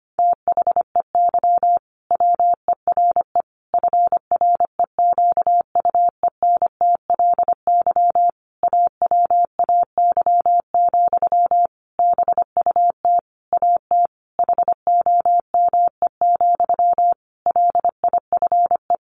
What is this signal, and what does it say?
Morse code, heard as a keyed tone, about 25 wpm, 700 Hz. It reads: THEY WERE FREQUENTLY AWAY, BUT AT HOME, LIFE
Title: Sentences from Top 1400 Words 25wpm